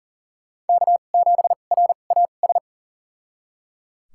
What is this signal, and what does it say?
Morse code, heard as a keyed tone, about 38 wpm, spaced as for 28 wpm, 700 Hz. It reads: K7RAS
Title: USA Call Signs - 5 Characters - 2x Character Spacing - 40wpm